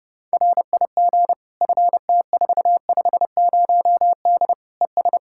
Morse code, heard as a keyed tone, about 31 wpm, 700 Hz. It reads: RIG FT450D ES
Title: Practice QSO - Exchange 2 - Info - 30wpm